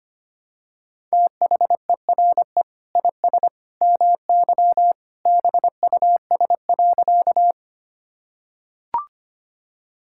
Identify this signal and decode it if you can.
Morse code, heard as a keyed tone, about 25 wpm, 700 Hz. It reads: THERE IS MY BUS.
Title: Sentences from Top 1000 Words - Review Part 6 - 25wpm